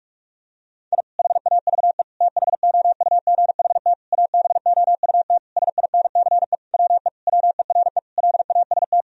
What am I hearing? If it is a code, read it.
Morse code, heard as a keyed tone, about 41 wpm, 700 Hz. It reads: I HAVE THOUGHT ABOUT SINCE WE WERE LAST
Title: Sentences from Top 1000 Words - Review Part 1 - 45wpm